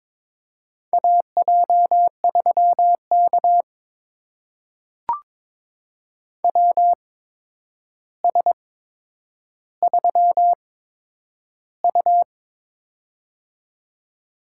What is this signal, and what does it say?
Morse code, heard as a keyed tone, about 22 wpm, spaced as for 9 wpm, 700 Hz. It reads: AJ3K WS3U
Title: USA Call Signs - 4 Characters - 8x Character Spacing - 22wpm